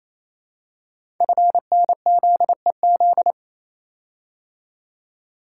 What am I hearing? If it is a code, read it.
Morse code, heard as a keyed tone, about 27 wpm, 700 Hz. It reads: FNZEZ
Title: Code Groups - 5 Letters - 28wpm